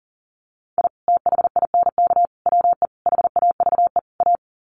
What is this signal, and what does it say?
Morse code, heard as a keyed tone, about 40 wpm, 700 Hz. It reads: I THINK WE HAVE A
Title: Sets of 5 Words Encore 40wpm